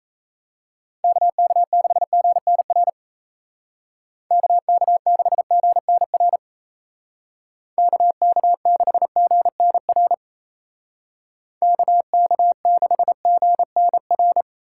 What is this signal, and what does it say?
Morse code, heard as a keyed tone, about 34 wpm, 700 Hz. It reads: KK6GNR KK6GNR KK6GNR KK6GNR
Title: Speed Racing - USA 2x3 Call Signs 28wpm